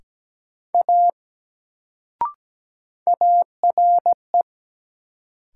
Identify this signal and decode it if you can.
Morse code, heard as a keyed tone, about 17 wpm, 700 Hz. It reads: A ARE
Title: Sentences from Top 100 Words - New Word Drill - 17wpm